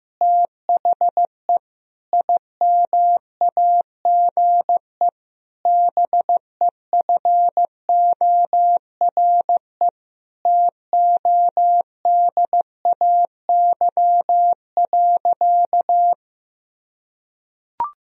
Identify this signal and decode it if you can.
Morse code, heard as a keyed tone, about 15 wpm, 700 Hz. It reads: THE IMAGE BEFORE TODAY.
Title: Sentences from Top 1000 Words - Review Part 4 - 15wpm